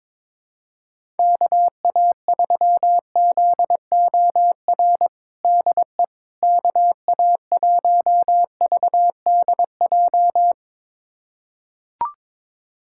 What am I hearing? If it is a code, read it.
Morse code, heard as a keyed tone, about 22 wpm, 700 Hz. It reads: KA3ZOR DE KA1VDJ
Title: USA Callsign DE Callsign - 6 Characters - 22wpm